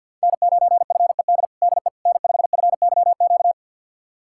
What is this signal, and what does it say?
Morse code, heard as a keyed tone, about 48 wpm, 700 Hz. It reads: N9FED DE N5FYQ
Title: USA Callsign DE Callsign - 5 Characters - 50wpm